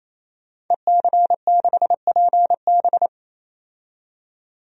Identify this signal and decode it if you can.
Morse code, heard as a keyed tone, about 28 wpm, 700 Hz. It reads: EC6PB
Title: DXCC-100 Call Signs - 5 Characters - 28wpm